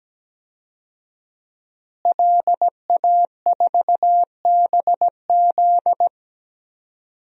Morse code, heard as a keyed tone, about 17 wpm, 700 Hz. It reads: LA4BZ
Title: DXCC-100 Call Signs - 5 Characters - Repeated 2x - 17wpm